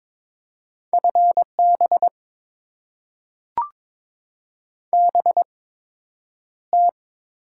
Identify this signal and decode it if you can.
Morse code, heard as a keyed tone, about 22 wpm, spaced as for 9 wpm, 700 Hz. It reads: FB BT
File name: Top 20 QSO Elements - 8x Character Spacing - 22wpm